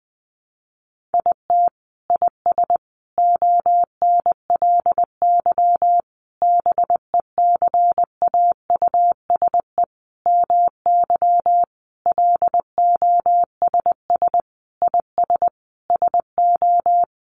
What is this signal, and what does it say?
Morse code, heard as a keyed tone, about 20 wpm, 700 Hz. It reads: IT IS ONLY BECAUSE MY LOSS IS SO
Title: Sentences from Top 1400 Words 20wpm